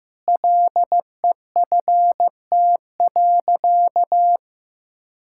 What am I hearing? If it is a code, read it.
Morse code, heard as a keyed tone, about 16 wpm, 700 Hz. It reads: LEFT.
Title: Sentences from Top 1600 Words 15wpm